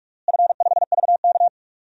Morse code, heard as a keyed tone, about 45 wpm, 700 Hz. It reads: F5VK
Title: DXCC-100 Call Signs - 4 Characters - Repeated 2x - 45wpm